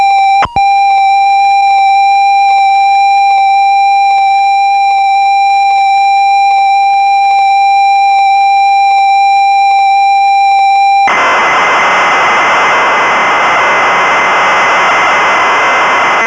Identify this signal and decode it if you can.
Computer data on cassette.